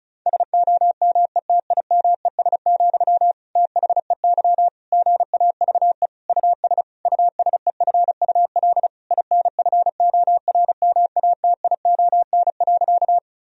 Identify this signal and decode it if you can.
Morse code, heard as a keyed tone, about 36 wpm, 700 Hz. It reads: SOMETIMES, THEY GAVE US USEFUL INFORMATION.
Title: Sentences from Top 1000 Words - Review Part 4 - 35wpm